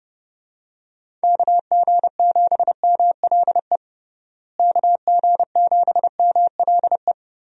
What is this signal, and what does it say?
Morse code, heard as a keyed tone, about 29 wpm, 700 Hz. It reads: KG7MLE KG7MLE
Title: USA Call Signs - 6 Characters - Repeated 2x - 30wpm